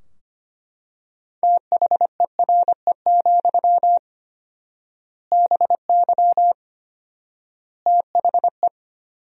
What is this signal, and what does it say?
Morse code, heard as a keyed tone, about 25 wpm, 700 Hz. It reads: THERE, BY THE
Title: 4x Word-Spacing - Sentences from Top 500 Words - 25wpm